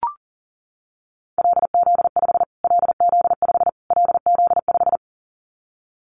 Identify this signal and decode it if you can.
Morse code, heard as a keyed tone, about 40 wpm, 700 Hz. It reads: LZ5 LZ5 LZ5
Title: Callsign Prefixes for Top 25 DXCC - Repeated 3x - 40wpm